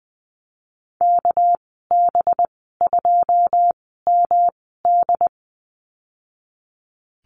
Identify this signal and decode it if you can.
Morse code, heard as a keyed tone, about 20 wpm, 700 Hz. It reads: K B 2 M D
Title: 231 Lesson 37g - Letter x - Call Signs 20wpm